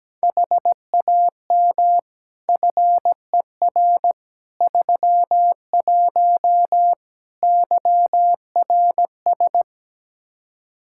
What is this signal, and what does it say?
Morse code, heard as a keyed tone, about 17 wpm, 700 Hz. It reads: HAM FER 31 YRS
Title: K1USN SST QSO Practice 17wpm